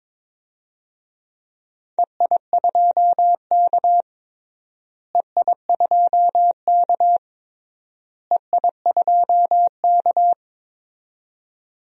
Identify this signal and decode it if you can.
Morse code, heard as a keyed tone, about 22 wpm, 700 Hz. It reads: EI2K EI2K EI2K
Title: DXCC-100 Call Signs - 4 Characters - Repeated 3x - 22wpm